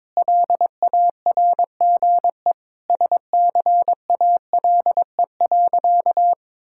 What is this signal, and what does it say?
Morse code, heard as a keyed tone, about 22 wpm, 700 Hz. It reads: LARGE SCALE.
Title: Sentences from Top 1000 Words - Review Part 5 - 22wpm